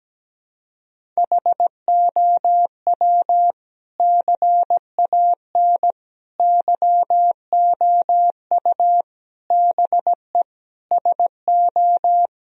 Morse code, heard as a keyed tone, about 17 wpm, 700 Hz. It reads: HOW CAN YOU BE SO
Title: Sentences from Top 1000 Words - Review Part 1 - 17wpm